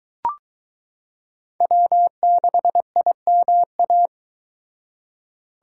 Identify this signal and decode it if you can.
Morse code, heard as a keyed tone, about 22 wpm, 700 Hz. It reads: W6IMA